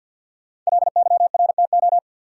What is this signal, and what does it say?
Morse code, heard as a keyed tone, about 52 wpm, 700 Hz. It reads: LYRTO
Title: Code Groups - 5 Letters - 50wpm